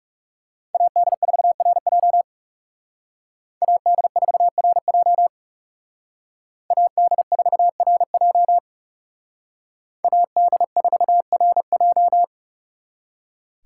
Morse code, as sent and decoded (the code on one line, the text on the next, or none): .- -.. ....- .-. .--- / .- -.. ....- .-. .--- / .- -.. ....- .-. .--- / .- -.. ....- .-. .---
AD4RJ AD4RJ AD4RJ AD4RJ